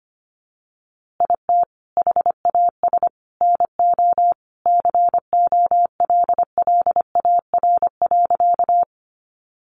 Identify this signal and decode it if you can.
Morse code, heard as a keyed tone, about 25 wpm, 700 Hz. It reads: IT HAS NO COLLAR.
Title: Easy Phrases 25wpm